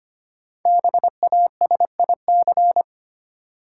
Morse code, heard as a keyed tone, about 25 wpm, 700 Hz. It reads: BASIC
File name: 5 Letter Words - First 3 Letters Given - 25wpm